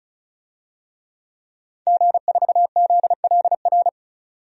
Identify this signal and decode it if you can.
Morse code, heard as a keyed tone, about 34 wpm, 700 Hz. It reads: G4ZLR
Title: DXCC-100 Call Signs - 5 Characters - 35wpm